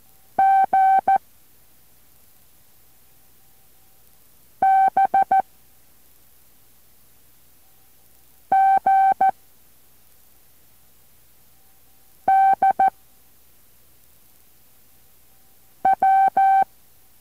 Morse code, heard as a keyed tone, about 14 wpm, 770 Hz. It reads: G B G D W